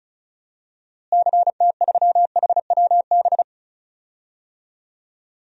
Morse code, heard as a keyed tone, about 34 wpm, 700 Hz. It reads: CT3HWB
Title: DXCC-100 Call Signs - 6 Characters - Repeated 3x - 35wpm